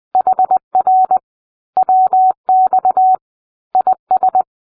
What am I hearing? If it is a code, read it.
Morse code, heard as a keyed tone, about 20 wpm, 750 Hz. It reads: HR WX IS